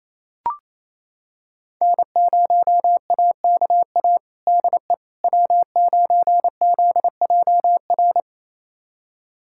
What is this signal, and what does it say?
Morse code, heard as a keyed tone, about 28 wpm, 700 Hz. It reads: N0AKA DE W9ZJR
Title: USA Callsign DE Callsign - 5 Characters - 28wpm